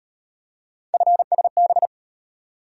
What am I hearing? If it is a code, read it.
Morse code, heard as a keyed tone, about 36 wpm, 700 Hz. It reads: FSB